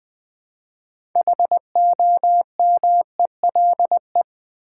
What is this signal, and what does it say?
Morse code, heard as a keyed tone, about 20 wpm, 700 Hz. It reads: HOMELE
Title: Build A Word - 2 Letters At A Time - 8-Letter Words - 20wpm